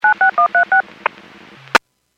DTMF, heard as telephone touch tones, tones of about 110 ms. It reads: 96136
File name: WPS / 01939 11-33 WPS 451.25 WPS